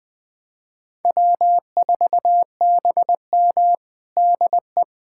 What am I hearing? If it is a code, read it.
Morse code, heard as a keyed tone, about 20 wpm, 700 Hz. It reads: W4BM DE
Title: USA Callsign DE Callsign - 4 Characters - 20wpm